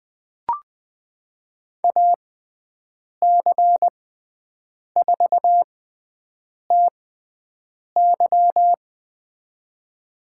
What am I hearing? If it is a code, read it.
Morse code, heard as a keyed tone, about 20 wpm, spaced as for 7 wpm, 700 Hz. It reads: AC4TY
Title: USA Call Signs - 6x Character Spacing - 20wpm